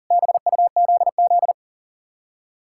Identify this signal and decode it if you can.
Morse code, heard as a keyed tone, about 52 wpm, 700 Hz. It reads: BUZZ